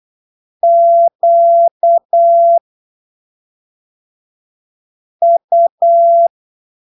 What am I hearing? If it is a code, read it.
Morse code, heard as a keyed tone, about 8 wpm, spaced as for 3 wpm, 680 Hz. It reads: QU
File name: arquivos / texto / texto 26 08ppm